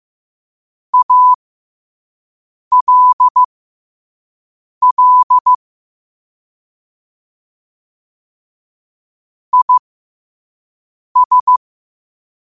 Morse code, heard as a keyed tone, about 15 wpm, spaced as for 5 wpm, 1000 Hz. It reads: ALL IS
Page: New quotes every day in morse code at 5 Words per minute.